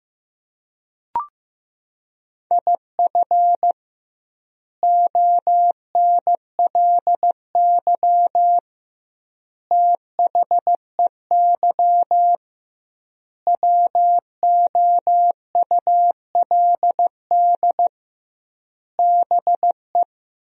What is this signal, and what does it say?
Morse code, heard as a keyed tone, about 15 wpm, 700 Hz. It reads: IF ONLY THEY WOULD BE
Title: 2x Word-Spacing - Sentences from Top 500 Words - 15wpm